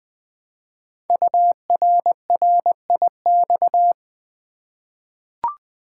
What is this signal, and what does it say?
Morse code, heard as a keyed tone, about 20 wpm, 700 Hz. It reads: URRIX
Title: Code Groups - 5 Letters - 20wpm